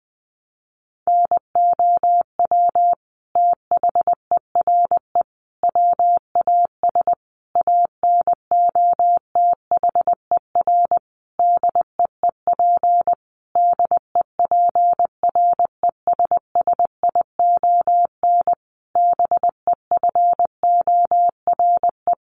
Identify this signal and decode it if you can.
Morse code, heard as a keyed tone, about 20 wpm, 700 Hz. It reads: NOW THERE WAS ANOTHER DEEP DEPRESSION BEFORE
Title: Sentences from Top 1300 Words 20wpm